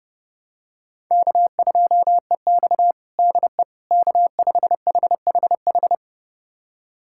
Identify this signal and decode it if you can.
Morse code, heard as a keyed tone, about 30 wpm, 700 Hz. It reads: K2EX DE K5HHH
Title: USA Callsign DE Callsign 30wpm